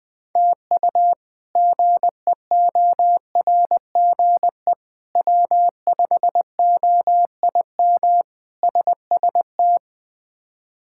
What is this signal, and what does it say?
Morse code, heard as a keyed tone, about 21 wpm, 700 Hz. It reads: TU GEORGE W5OIM SST
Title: K1USN SST QSO Practice - 20wpm